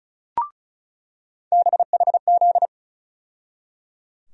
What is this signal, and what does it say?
Morse code, heard as a keyed tone, about 36 wpm, 700 Hz. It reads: BHZ
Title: USA Call Sign Postfixes - 3 Letters 35wpm